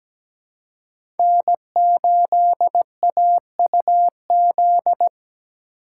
Morse code, heard as a keyed tone, about 17 wpm, 700 Hz. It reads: N8AUZ
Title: USA Call Signs - 4x Character Spacing - 17wpm